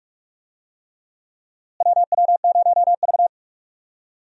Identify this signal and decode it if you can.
Morse code, heard as a keyed tone, about 46 wpm, 700 Hz. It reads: WW0V
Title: USA Call Signs - 4 Characters - Repeated 3x - 45wpm